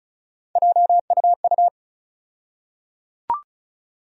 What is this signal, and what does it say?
Morse code, heard as a keyed tone, about 36 wpm, 700 Hz. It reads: JUU
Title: USA Call Sign Postfixes - 3 Letters 35wpm